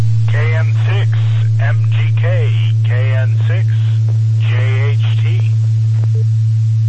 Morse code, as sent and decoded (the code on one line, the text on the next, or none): none